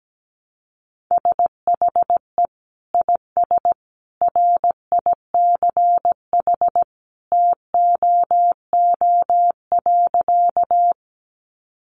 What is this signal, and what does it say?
Morse code, heard as a keyed tone, about 17 wpm, 700 Hz. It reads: SHE IS RICH TOO.
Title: Sentences from Top 1100 Words 17wpm